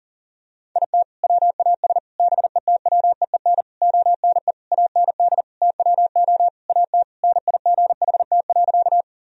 Morse code, heard as a keyed tone, about 40 wpm, 700 Hz. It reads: IT WAS BETWEEN ONE AND TWO AT NIGHT.
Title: Sentences from Top 1000 Words - Review Part 1 - 40wpm